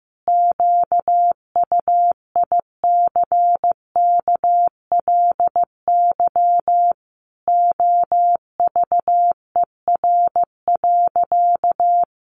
Morse code, heard as a keyed tone, about 15 wpm, 700 Hz. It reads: QUICKLY OVER.
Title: Sentences from Top 1100 Words 15wpm